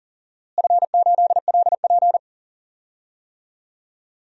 Morse code, heard as a keyed tone, about 40 wpm, 700 Hz. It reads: F8LP
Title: DXCC-100 Call Signs - 4 Characters - Repeated 2x - 40wpm